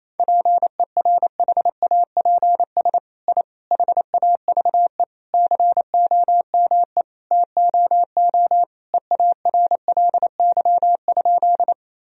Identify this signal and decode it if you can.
Morse code, heard as a keyed tone, about 27 wpm, 700 Hz. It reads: PERHAPS I HAVE COME TOO EARLY?
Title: Sentences from Top 1000 Words - Review Part 2 - 28wpm